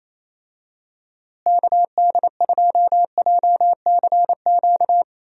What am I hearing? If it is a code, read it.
Morse code, heard as a keyed tone, about 28 wpm, 700 Hz. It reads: KD2JCQ
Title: USA Call Signs - 6 Characters - Repeated 3x - 28wpm